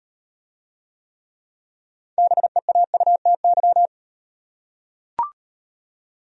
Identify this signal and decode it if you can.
Morse code, heard as a keyed tone, about 38 wpm, 700 Hz. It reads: BEAUTY